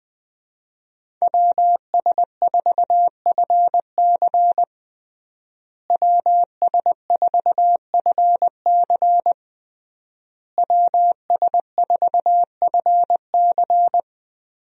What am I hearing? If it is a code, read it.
Morse code, heard as a keyed tone, about 21 wpm, 700 Hz. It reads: WS4FC WS4FC WS4FC
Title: USA Call Signs - 5 Characters - Repeated 3x - 20wpm